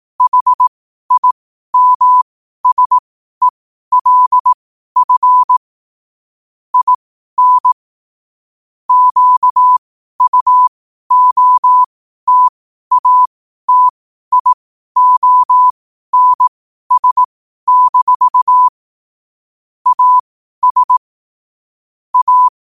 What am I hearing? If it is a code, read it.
Morse code, heard as a keyed tone, about 18 wpm, spaced as for 12 wpm, 1000 Hz. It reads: HIMSELF IN QUOTATIONS- AS A
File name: qotdfiles / 12wpm